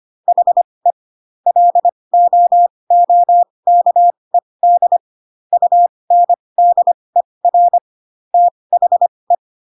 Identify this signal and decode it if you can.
Morse code, heard as a keyed tone, about 25 wpm, spaced as for 20 wpm, 700 Hz. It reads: HE LOOKED UNDER THE
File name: Pinocchio / Pinocchio ch1 15 25 ik1hge